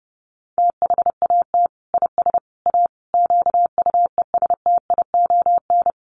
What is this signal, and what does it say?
Morse code, heard as a keyed tone, about 30 wpm, 700 Hz. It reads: THAT IS A QUESTION
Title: Sentences from Top 600 Words 30wpm